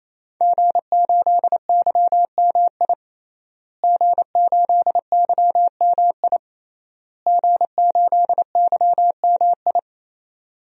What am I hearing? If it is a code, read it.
Morse code, heard as a keyed tone, about 27 wpm, 700 Hz. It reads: G8YMI G8YMI G8YMI
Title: DXCC-100 Call Signs - 5 Characters - Repeated 3x - 28wpm